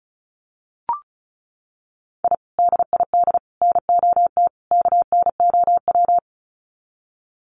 Morse code, heard as a keyed tone, about 34 wpm, 700 Hz. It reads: I DID NOT KNOW
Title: Sets of 4 Words 35wpm